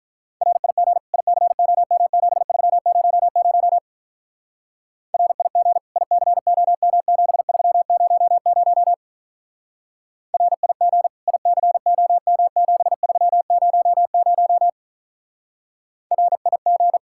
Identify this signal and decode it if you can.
Morse code, heard as a keyed tone, about 46 wpm, 700 Hz. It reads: RIG ICOM7300 RIG ICOM7300 RIG ICOM7300 RIG